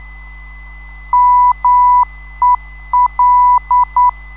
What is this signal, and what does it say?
Morse code, heard as a keyed tone, about 9 wpm, 990 Hz. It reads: MEL
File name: ndb zvok / Mel